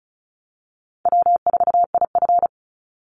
Morse code, heard as a keyed tone, about 34 wpm, 700 Hz. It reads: W4IF
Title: Easy Call Signs Repeated 3x - 35wpm